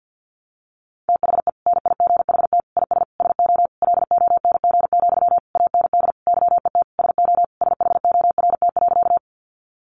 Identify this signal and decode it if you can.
Morse code, heard as a keyed tone, about 50 wpm, 700 Hz. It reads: THE NIGHT IS SO LONG, AND YET SO SHORT.
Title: Sentences from Top 500 Words 50wpm